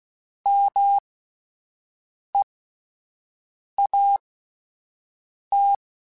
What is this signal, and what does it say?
Morse code, heard as a keyed tone, about 16 wpm, spaced as for 5 wpm, 800 Hz. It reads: MEAT
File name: Alphabetical / Alphabetical mp3s / Ms @5